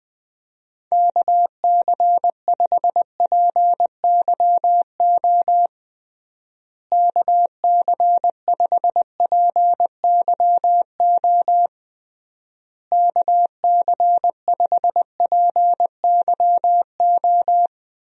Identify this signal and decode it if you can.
Morse code, heard as a keyed tone, about 21 wpm, 700 Hz. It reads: KC5PYO KC5PYO KC5PYO
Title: USA Call Signs - 6 Characters - Repeated 3x - 20wpm